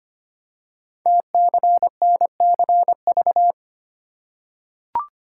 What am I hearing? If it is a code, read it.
Morse code, heard as a keyed tone, about 25 wpm, 700 Hz. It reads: TCNCV
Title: Code Groups - 5 Letters - 25wpm